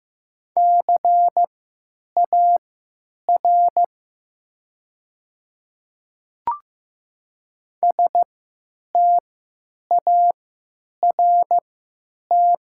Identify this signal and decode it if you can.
Morse code, heard as a keyed tone, about 15 wpm, spaced as for 9 wpm, 700 Hz. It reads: CAR START
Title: 3x Character Spacing - Top 500 Words - 15wpm